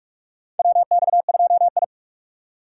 Morse code, heard as a keyed tone, about 43 wpm, 700 Hz. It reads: WX2I
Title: USA Call Signs - 3x Character Spacing - 45wpm